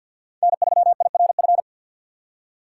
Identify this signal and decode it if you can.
Morse code, heard as a keyed tone, about 47 wpm, 700 Hz. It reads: N3IRF
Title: USA Call Signs - 4x Character Spacing - 50wpm